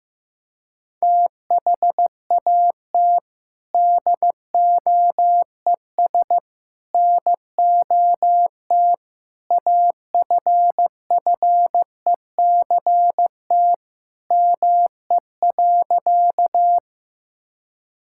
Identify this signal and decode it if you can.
Morse code, heard as a keyed tone, about 16 wpm, 700 Hz. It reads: THAT DOES NOT AFFECT ME.
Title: Sentences from Top 1000 Words - Review Part 6 - 15wpm